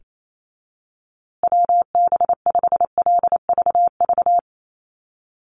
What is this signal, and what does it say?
Morse code, heard as a keyed tone, about 28 wpm, 700 Hz. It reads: WB5LVV
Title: USA Call Signs Repeated 2x 28wpm